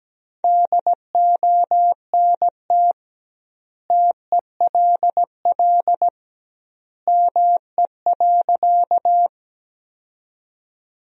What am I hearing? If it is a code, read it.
Morse code, heard as a keyed tone, about 17 wpm, 700 Hz. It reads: DONT TELL ME.
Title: 2x Word-Spacing - Sentences from Top 300 Words - 17wpm